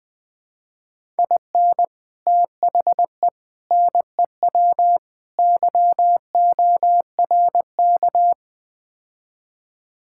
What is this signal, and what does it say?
Morse code, heard as a keyed tone, about 21 wpm, 700 Hz. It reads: IN THE NEW YORK
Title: Build A Phrase - 5-Word Phrases - 20wpm